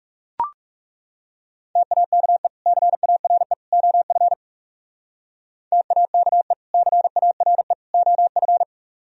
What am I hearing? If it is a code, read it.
Morse code, heard as a keyed tone, about 36 wpm, 700 Hz. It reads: TAKE CARE OF TAKE CARE OF
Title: Speed-Racing - Sets of 3 Words 30wpm